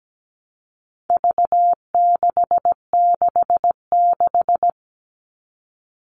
Morse code, heard as a keyed tone, about 17 wpm, 700 Hz. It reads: V666
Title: All Cut Numbers - 4-Digit 17wpm